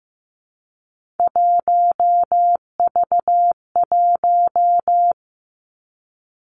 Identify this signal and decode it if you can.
Morse code, heard as a keyed tone, about 15 wpm, 700 Hz. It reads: 1V1
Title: Three Challenging Letter-Number Combinations 15wpm